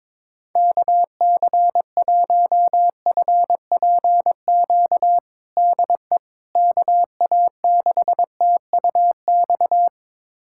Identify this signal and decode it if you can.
Morse code, heard as a keyed tone, about 21 wpm, 700 Hz. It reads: KC1FPQ DE KA6TUX
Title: USA Callsign DE Callsign 22wpm